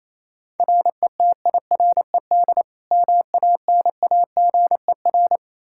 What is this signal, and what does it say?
Morse code, heard as a keyed tone, about 27 wpm, 700 Hz. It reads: RETIRED MANAGER
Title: Practice QSO - Exchange 3 - More Info - 28wpm